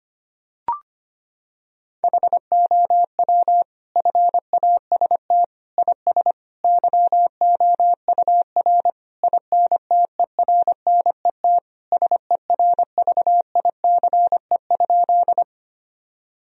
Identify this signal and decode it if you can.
Morse code, heard as a keyed tone, about 25 wpm, 700 Hz. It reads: HOW FAST IS YOUR INTERNET SERVICE?
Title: Sentences from Top 1000 Words - Review Part 4 - 25wpm